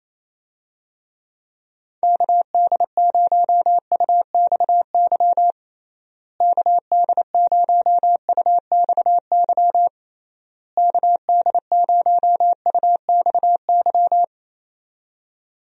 Morse code, heard as a keyed tone, about 28 wpm, 700 Hz. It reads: KD0UXY KD0UXY KD0UXY
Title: USA Call Signs - 6 Characters - Repeated 3x - 28wpm